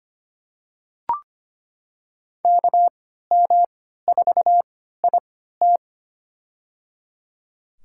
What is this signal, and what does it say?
Morse code, heard as a keyed tone, about 25 wpm, spaced as for 14 wpm, 700 Hz. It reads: KM4IT
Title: USA Call Signs - 3x Character Spacing - 25wpm